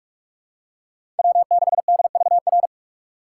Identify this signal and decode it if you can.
Morse code, heard as a keyed tone, about 44 wpm, 700 Hz. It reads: W6DVR
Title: USA Call Signs - 5 Characters - 6x Character Spacing - 45wpm